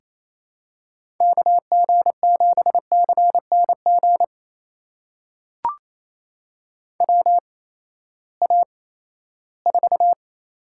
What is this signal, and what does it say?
Morse code, heard as a keyed tone, about 28 wpm, spaced as for 16 wpm, 700 Hz. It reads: KG7CNG WA4
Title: USA Call Signs - 6 Characters - 8x Character Spacing - 28wpm